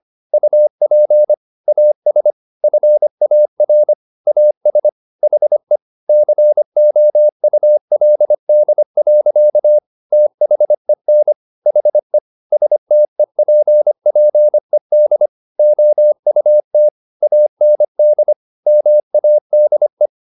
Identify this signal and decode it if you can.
Morse code, heard as a keyed tone, about 25 wpm, 600 Hz. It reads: UP AS FAR AS HE COULD. THEN HE STEPPED OUT AND MADE